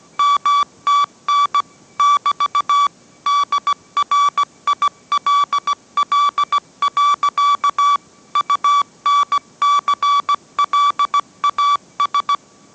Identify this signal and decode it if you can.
Morse code, heard as a keyed tone, about 17 wpm, 1150 Hz. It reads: MTN <BT> DRILL. UNCLAS